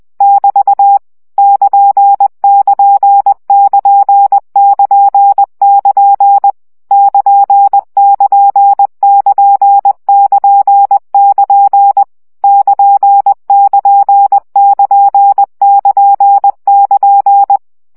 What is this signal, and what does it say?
Morse code, heard as a keyed tone, about 20 wpm, 800 Hz. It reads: <BT> <KN><KN><KN><KN><KN> <KN><KN><KN><KN><KN> <KN><KN><KN><KN><KN>
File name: CW gyakorlas kn 17 baud